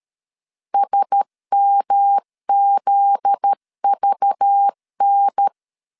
Morse code, heard as a keyed tone, about 13 wpm, 800 Hz. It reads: SMZVN
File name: LEZIONI / 7 GF9Z / 7 GF9Z 12wpm 2147